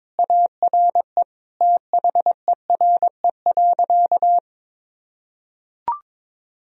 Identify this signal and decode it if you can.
Morse code, heard as a keyed tone, about 22 wpm, 700 Hz. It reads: ARE THERE.
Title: Sentences from Top 1000 Words - Review Part 1 - 22wpm